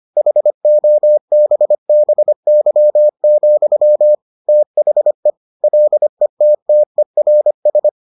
Morse code, heard as a keyed tone, about 25 wpm, 600 Hz. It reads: HOBBY, THE LETTERS